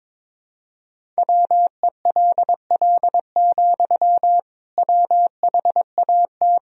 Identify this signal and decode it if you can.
Morse code, heard as a keyed tone, about 22 wpm, 700 Hz. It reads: WELL, WHAT
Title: Sentences from Top 200 Words and Abbreviations 22wpm